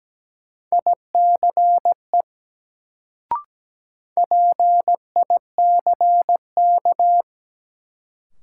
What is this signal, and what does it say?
Morse code, heard as a keyed tone, about 17 wpm, 700 Hz. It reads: ICE PICK
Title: Sentences from Top 1000 Words - New Word Drill - 17wpm